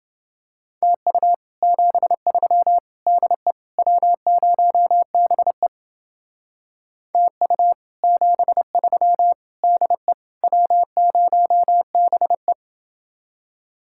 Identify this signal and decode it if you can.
Morse code, heard as a keyed tone, about 29 wpm, 700 Hz. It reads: TU 73 DE W0BE TU 73 DE W0BE